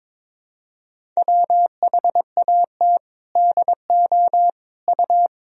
Morse code, heard as a keyed tone, about 21 wpm, 700 Hz. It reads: WHAT DO U
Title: Sentences from Top 300 Words and Abbreviations 22wpm